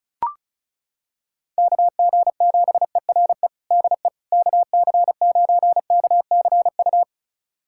Morse code, heard as a keyed tone, about 36 wpm, 700 Hz. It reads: KG7ERE DE KC9KCU
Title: USA Callsign DE Callsign 35wpm